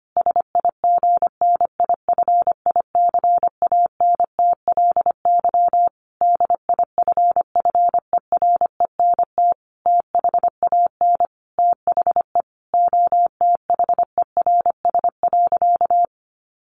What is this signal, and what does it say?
Morse code, heard as a keyed tone, about 25 wpm, 700 Hz. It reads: SIGNIFICANTLY DIFFERENT THAN THE OTHERS.
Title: Sentences from Top 1500 Words 25wpm